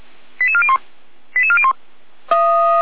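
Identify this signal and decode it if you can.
Five-tone selective call (ZVEI), heard as quick runs of five tones, tones of about 70 ms. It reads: 88401 88401
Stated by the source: ZVEI five-tone call received on SDR.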